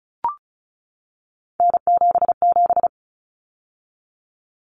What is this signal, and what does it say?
Morse code, heard as a keyed tone, about 34 wpm, 700 Hz. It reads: N77
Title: Common Cut Numbers - 3-Digit 35wpm